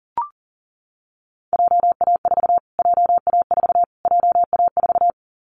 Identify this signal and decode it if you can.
Morse code, heard as a keyed tone, about 37 wpm, 700 Hz. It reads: JA4 JA4 JA4
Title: Callsign Prefixes for Top 25 DXCC - Repeated 3x - 40wpm